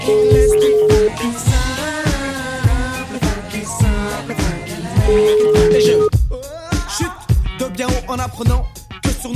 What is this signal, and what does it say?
Ringback tone (Europe, 425 Hz), with noise.